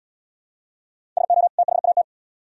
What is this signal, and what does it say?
Morse code, heard as a keyed tone, about 77 wpm, 700 Hz. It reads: SO THAT